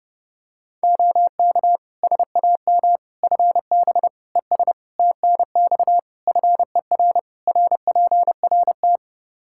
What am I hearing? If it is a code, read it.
Morse code, heard as a keyed tone, about 30 wpm, 700 Hz. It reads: OK SAM FB ES TNX FER RPRT
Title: Practice QSO - Exchange 2 - Info - 30wpm